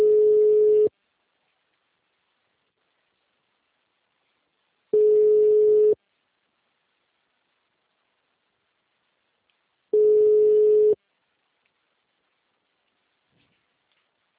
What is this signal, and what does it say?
Ringback tone (Europe, 425 Hz).